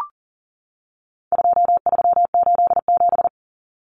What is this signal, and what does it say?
Morse code, heard as a keyed tone, about 40 wpm, 700 Hz. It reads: 2387
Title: Common Cut Numbers - 4-Digit 40wpm